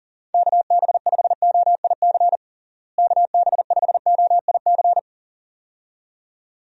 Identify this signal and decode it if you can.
Morse code, heard as a keyed tone, about 37 wpm, 700 Hz. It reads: KB5OIC KB5OIC
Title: USA Call Signs - 6 Characters - Repeated 2x - 40wpm